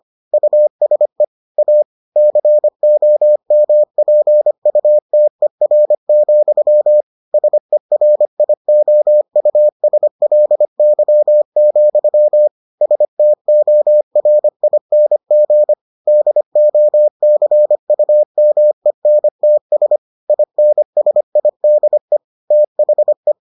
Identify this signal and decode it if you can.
Morse code, heard as a keyed tone, about 25 wpm, 600 Hz. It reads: USE A COMPUTER, SERIOUSLY, STORING DOCUMENTS INSIDE THE